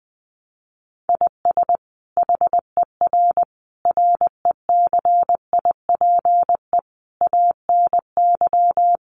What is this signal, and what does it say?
Morse code, heard as a keyed tone, about 20 wpm, 700 Hz. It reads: IS HER RECIPE ANY
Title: Sentences from Top 1200 Words 20wpm